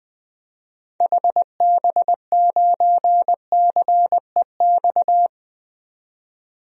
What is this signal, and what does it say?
Morse code, heard as a keyed tone, about 20 wpm, 700 Hz. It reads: HB9CEX
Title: DXCC-100 Call Signs - 6 Characters - 20wpm